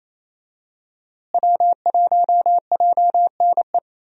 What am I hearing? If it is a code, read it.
Morse code, heard as a keyed tone, about 28 wpm, 700 Hz. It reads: W1JNE
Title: USA Call Signs - 5 Characters - Repeated 3x - 28wpm